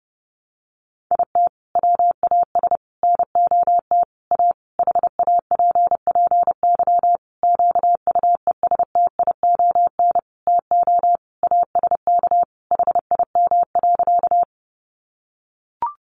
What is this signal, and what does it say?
Morse code, heard as a keyed tone, about 28 wpm, 700 Hz. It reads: IT WAS NOT A HAPPY QUESTION TO ASK HIM.
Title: Sentences from Top 1000 Words 30wpm